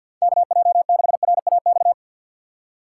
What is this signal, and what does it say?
Morse code, heard as a keyed tone, about 47 wpm, 700 Hz. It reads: KJ6RAX
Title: USA Call Signs - 6 Characters - 3x Character Spacing - 50wpm